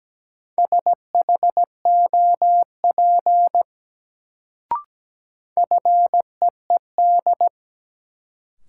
Morse code, heard as a keyed tone, about 17 wpm, 700 Hz. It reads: SHOP FEED
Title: Sentences from Top 1600 Words - New Word Drill Alt - 17wpm